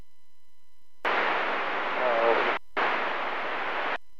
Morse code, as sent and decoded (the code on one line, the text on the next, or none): none